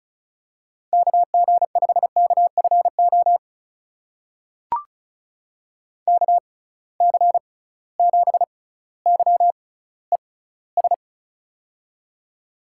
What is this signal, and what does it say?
Morse code, heard as a keyed tone, about 36 wpm, spaced as for 18 wpm, 700 Hz. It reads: KG5KFO KC7YES
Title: USA Call Signs - 6 Characters - 6x Character Spacing - 35wpm